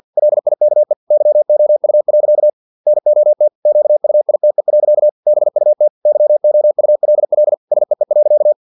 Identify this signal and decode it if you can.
Morse code, heard as a keyed tone, about 50 wpm, 600 Hz. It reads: LIKE YOU. NOT QUITE. BUT YOULL SEE.
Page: Never 49wpm